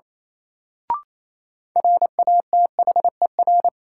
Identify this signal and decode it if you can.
Morse code, heard as a keyed tone, about 28 wpm, 700 Hz. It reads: RATHER
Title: Sets of 2 Words - New Word Drill - 28wpm